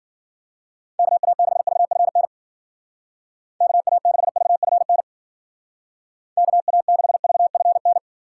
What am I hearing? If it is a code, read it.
Morse code, heard as a keyed tone, about 51 wpm, 700 Hz. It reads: KA6VFN KA6VFN KA6VFN